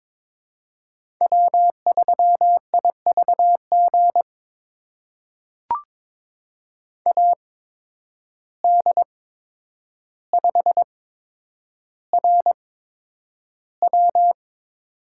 Morse code, heard as a keyed tone, about 22 wpm, spaced as for 9 wpm, 700 Hz. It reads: W3IVG AD5RW
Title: USA Call Signs - 5 Characters - 8x Character Spacing - 22wpm